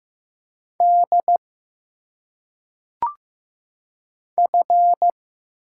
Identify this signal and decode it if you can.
Morse code, heard as a keyed tone, about 15 wpm, 700 Hz. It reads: D F
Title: Single Letters - Phonetic Pronunciation 15wpm